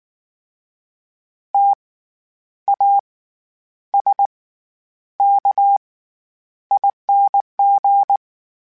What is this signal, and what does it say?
Morse code, heard as a keyed tone, about 19 wpm, spaced as for 9 wpm, 800 Hz. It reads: TASKING
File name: Special INGs / Special INGs-mp3s / SlowToFaster ing @18